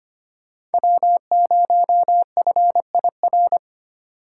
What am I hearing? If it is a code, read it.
Morse code, heard as a keyed tone, about 30 wpm, 700 Hz. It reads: W0FIR
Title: Speed Racing - USA 1x3 Call Signs 25wpm